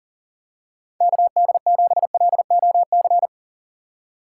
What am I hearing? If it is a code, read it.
Morse code, heard as a keyed tone, about 37 wpm, 700 Hz. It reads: KD7LOC
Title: USA Call Signs - 6 Characters - 3x Character Spacing - 40wpm